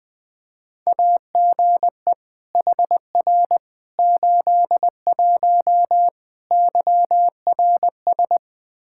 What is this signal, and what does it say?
Morse code, heard as a keyed tone, about 20 wpm, 700 Hz. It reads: AGE HR 81 YRS
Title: Practice QSO - Exchange 3 - More Info - 20wpm